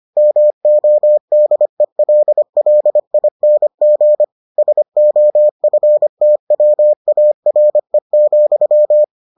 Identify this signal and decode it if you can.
Morse code, heard as a keyed tone, about 25 wpm, 600 Hz. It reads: MODELLING SOFTWARE,